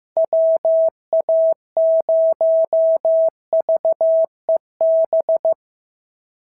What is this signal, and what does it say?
Morse code, heard as a keyed tone, about 15 wpm, 690 Hz. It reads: WA0VEB
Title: SOTA QSO Practice 15wpm